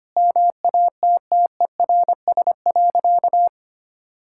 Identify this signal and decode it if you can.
Morse code, heard as a keyed tone, about 25 wpm, 700 Hz. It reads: MATTERS.
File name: practice / 25wpm / Sentences with 4-Word Phrases - 25wpm